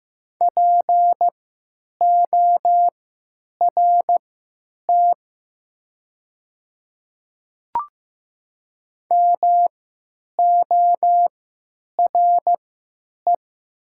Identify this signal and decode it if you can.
Morse code, heard as a keyed tone, about 15 wpm, spaced as for 9 wpm, 700 Hz. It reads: PORT MORE
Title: 3x Character Spacing - Top 200 Words - 15wpm